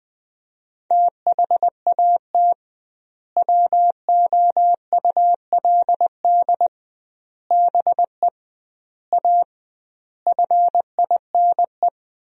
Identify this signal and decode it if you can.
Morse code, heard as a keyed tone, about 20 wpm, 700 Hz. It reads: THAT WOULD BE A FINE
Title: 2x Word-Spacing - Sentences from Top 500 Words - 20wpm